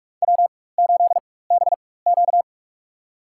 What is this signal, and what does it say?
Morse code, heard as a keyed tone, about 45 wpm, spaced as for 21 wpm, 700 Hz. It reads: W8BQ
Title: USA Call Signs - 4x Character Spacing - 45wpm